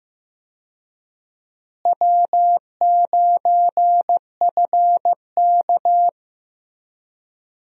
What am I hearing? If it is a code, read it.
Morse code, heard as a keyed tone, about 15 wpm, 700 Hz. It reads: W9FK
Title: USA Call Signs - 4 Characters - Repeated 2x - 15wpm